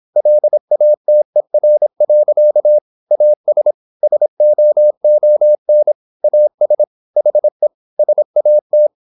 Morse code, heard as a keyed tone, about 26 wpm, 600 Hz. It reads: LATER. AS SOON AS HE SAT
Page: Never 26wpm | CW med Gnister